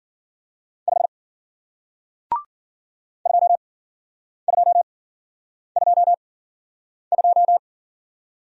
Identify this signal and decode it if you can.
Morse code, heard as a keyed tone, about 48 wpm, spaced as for 9 wpm, 700 Hz. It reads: 5 2222
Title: Speed-Racing Single Letter-Number 40wpm